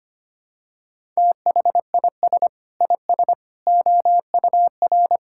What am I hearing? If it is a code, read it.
Morse code, heard as a keyed tone, about 25 wpm, 700 Hz. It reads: THIS IS OUR
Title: Sentences from Top 1000 Words - Review Part 5 - 25wpm